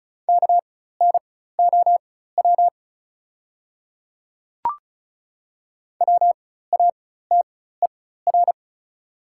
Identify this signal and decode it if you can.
Morse code, heard as a keyed tone, about 36 wpm, spaced as for 16 wpm, 690 Hz. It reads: KNOW WATER
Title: 4x Character Spacing - Top 100 Words - 35wpm